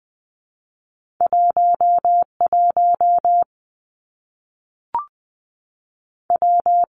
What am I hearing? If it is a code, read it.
Morse code, heard as a keyed tone, about 20 wpm, 700 Hz. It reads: 1 1 W
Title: Two Challenging Letter-Number Combinations 20wpm